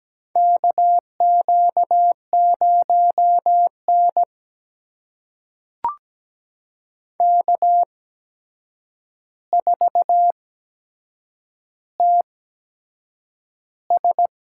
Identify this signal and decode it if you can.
Morse code, heard as a keyed tone, about 17 wpm, spaced as for 7 wpm, 700 Hz. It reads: KQ0N K4TS
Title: USA Call Signs - 4 Characters - 8x Character Spacing - 17wpm